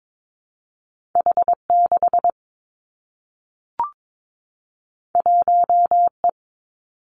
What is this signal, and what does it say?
Morse code, heard as a keyed tone, about 22 wpm, 700 Hz. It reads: H 6 1 E
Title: Two Challenging Letter-Number Combinations 22wpm